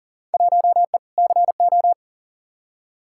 Morse code, heard as a keyed tone, about 40 wpm, 700 Hz. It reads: 1E CO
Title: Field Day Exchanges - 40wpm